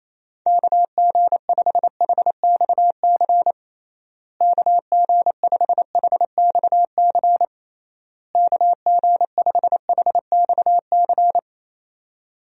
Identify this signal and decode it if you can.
Morse code, heard as a keyed tone, about 28 wpm, 700 Hz. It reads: KG5HXC KG5HXC KG5HXC
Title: USA Call Signs - 6 Characters - Repeated 3x - 28wpm